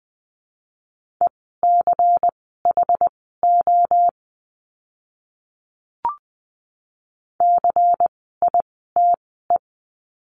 Morse code, heard as a keyed tone, about 20 wpm, spaced as for 14 wpm, 700 Hz. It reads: ECHO CITE
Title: 094 Lesson 16f - Letter c - Words 20wpm